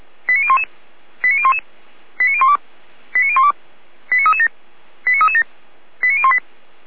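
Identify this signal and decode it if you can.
Five-tone selective call (ZVEI), heard as quick runs of five tones, tones of about 70 ms. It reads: 89011 89011 89012 89012 89337 89337 89018
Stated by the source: ZVEI five-tone selective call received on SDR.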